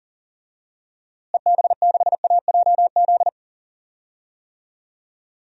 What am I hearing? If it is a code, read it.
Morse code, heard as a keyed tone, about 38 wpm, 700 Hz. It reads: EB6AJZ
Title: DXCC-100 Call Signs - 6 Characters - Repeated 3x - 40wpm